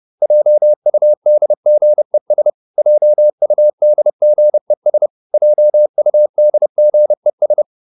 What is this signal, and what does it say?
Morse code, heard as a keyed tone, about 30 wpm, 600 Hz. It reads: JUDGES JUDGES JUDGES